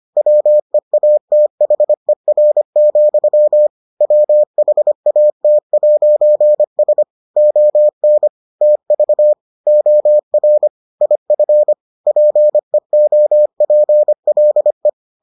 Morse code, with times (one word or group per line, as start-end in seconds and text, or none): 0.2-3.7 WEATHER,
4.0-7.0 WHAT'S
7.4-8.3 ON
8.6-9.3 TV
9.7-10.7 OR
11.0-11.7 IF
12.1-14.9 PEOPLE